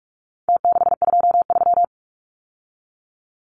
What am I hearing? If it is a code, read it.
Morse code, heard as a keyed tone, about 44 wpm, 700 Hz. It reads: T623
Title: Numbers - Contest Style - 4-Digit 45wpm